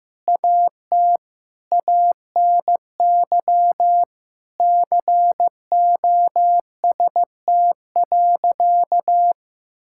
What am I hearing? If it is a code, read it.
Morse code, heard as a keyed tone, about 15 wpm, 700 Hz. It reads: AT ANY COST.
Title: Sentences from Top 1000 Words - Review Part 5 - 15wpm